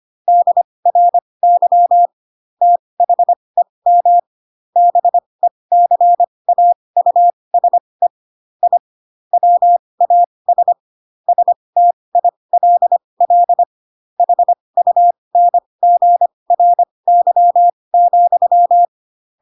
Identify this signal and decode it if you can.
Morse code, heard as a keyed tone, about 25 wpm, spaced as for 20 wpm, 700 Hz. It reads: DRY THEM BECAUSE I WAS STILL HUNGRY,
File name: Pinocchio / Pinocchio ch7 15 25 ik1hge